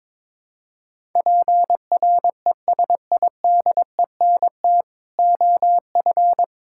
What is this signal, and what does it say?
Morse code, heard as a keyed tone, about 22 wpm, 700 Hz. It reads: PRESIDENT OF
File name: practice / 22wpm / Build A Phrase - 5-Word Phrases - 22wpm